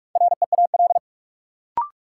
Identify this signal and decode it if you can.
Morse code, heard as a keyed tone, about 41 wpm, 700 Hz. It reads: REAL
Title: Sentences from Top 300 Words - New Word Drill - 45wpm